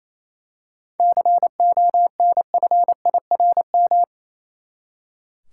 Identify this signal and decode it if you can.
Morse code, heard as a keyed tone, about 28 wpm, 700 Hz. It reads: CONFIRM
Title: Seven Letter Words - 28wpm